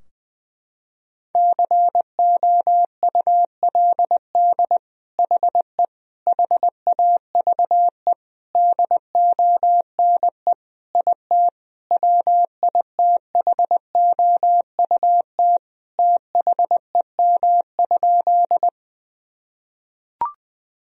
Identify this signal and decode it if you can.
Morse code, heard as a keyed tone, about 20 wpm, 700 Hz. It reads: COULD HE HAVE DONE IT WITHOUT THEM?
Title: Sentences from Top 1000 Words - Review Part 3 - 20wpm